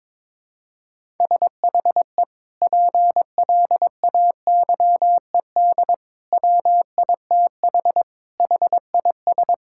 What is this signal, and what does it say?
Morse code, heard as a keyed tone, about 22 wpm, 700 Hz. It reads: SHE PLAYED WITH HIS
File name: practice / 22wpm / Sentences with 2-Word Phrases - 22wpm